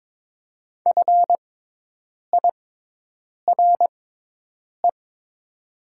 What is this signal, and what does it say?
Morse code, heard as a keyed tone, about 23 wpm, spaced as for 8 wpm, 700 Hz. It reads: FIRE
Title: 6x Character Spacing - Top 500 Words - 22wpm